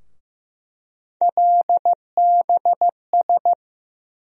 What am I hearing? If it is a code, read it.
Morse code, heard as a keyed tone, about 15 wpm, 700 Hz. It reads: LBS
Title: USA Call Sign Postfixes - 3 Letters 15wpm